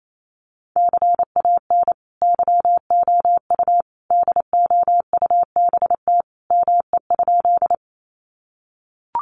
Morse code, heard as a keyed tone, about 28 wpm, 700 Hz. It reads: CAN YOU DOUBT ME?
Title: Sentences from Top 1500 Words 28wpm